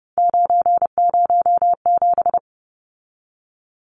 Morse code, heard as a keyed tone, about 30 wpm, 700 Hz. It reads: 907
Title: Numbers - 3-Digit 30wpm